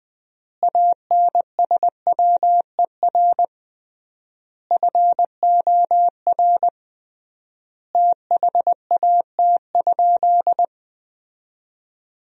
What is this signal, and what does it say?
Morse code, heard as a keyed tone, about 20 wpm, 700 Hz. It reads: ANSWER FOR THAT?
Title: 3x Word-Spacing - Sentences from Top 300 Words - 20wpm